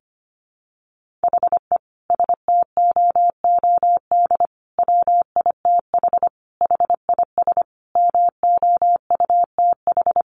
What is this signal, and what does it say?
Morse code, heard as a keyed tone, about 25 wpm, 700 Hz. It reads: HE STOOD WITH HIS MOUTH
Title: Sentences from Top 1500 Words 25wpm